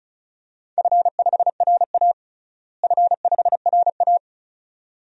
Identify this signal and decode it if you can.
Morse code, heard as a keyed tone, about 36 wpm, 700 Hz. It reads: F5RA F5RA
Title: DXCC-100 Call Signs - 4 Characters - Repeated 2x - 35wpm